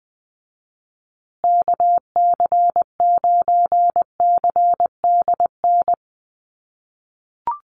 Morse code, heard as a keyed tone, about 20 wpm, 700 Hz. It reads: KC9CDN
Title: USA Call Signs Repeated 2x 20wpm